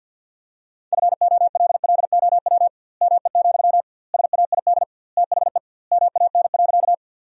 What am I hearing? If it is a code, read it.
Morse code, heard as a keyed tone, about 50 wpm, 700 Hz. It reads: FOLLOW ME, SAID THE MAN.
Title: 2x Word-Spacing - Sentences from Top 200 Words - 50wpm